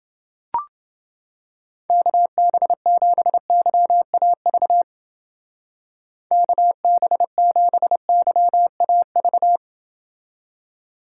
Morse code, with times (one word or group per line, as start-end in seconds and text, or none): 1.9-4.8 KB7YAV
6.3-9.6 KB7YAV